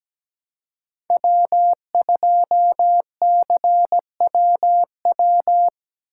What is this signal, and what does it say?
Morse code, heard as a keyed tone, about 17 wpm, 700 Hz. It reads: W2CWW
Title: USA Call Signs - 5 Characters - 4x Character Spacing - 17wpm